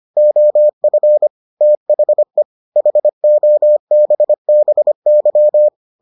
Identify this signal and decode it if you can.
Morse code, heard as a keyed tone, about 25 wpm, 600 Hz. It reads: OF THE HOBBY